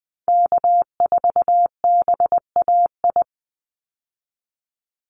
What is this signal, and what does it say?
Morse code, heard as a keyed tone, about 20 wpm, 700 Hz. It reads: K4BAI
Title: Call Signs Repeated 3x - 20wpm